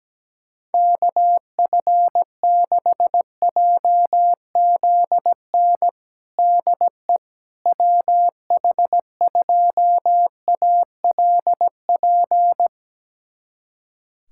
Morse code, heard as a keyed tone, about 17 wpm, 700 Hz. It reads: KF6JZN DE WH2ALP
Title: USA Callsign DE Callsign - 6 Characters - 17wpm